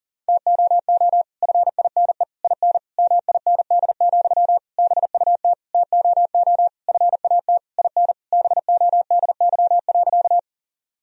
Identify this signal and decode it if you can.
Morse code, heard as a keyed tone, about 40 wpm, 700 Hz. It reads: TOO FINE IN MIND, BUT TOO FAT IN BODY.
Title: Sentences from Top 1000 Words - Review Part 4 - 40wpm